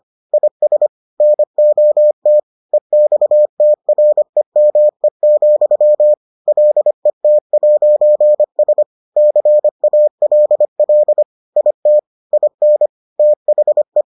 Morse code, heard as a keyed tone, about 25 wpm, 600 Hz. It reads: IS NOT EXTREME, LET'S CALL IT IN THE